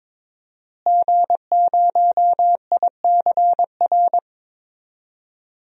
Morse code, heard as a keyed tone, about 21 wpm, 700 Hz. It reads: G0ICR
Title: DXCC-100 Call Signs - 5 Characters - Repeated 3x - 22wpm